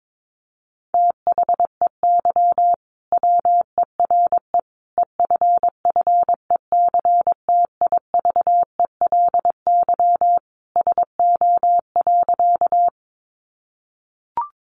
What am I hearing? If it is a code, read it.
Morse code, heard as a keyed tone, about 22 wpm, 700 Hz. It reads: THEY WERE EFFECTIVELY SO.
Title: Sentences from Top 1400 Words 22wpm